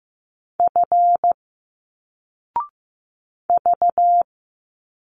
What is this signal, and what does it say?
Morse code, heard as a keyed tone, about 15 wpm, 700 Hz. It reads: F V
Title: Single Letters 15wpm